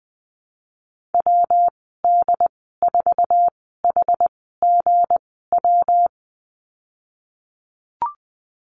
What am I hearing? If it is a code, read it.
Morse code, heard as a keyed tone, about 20 wpm, spaced as for 15 wpm, 700 Hz. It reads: WD4HGW
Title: 219 Lesson 35g - Number 0 - Call Signs 20wpm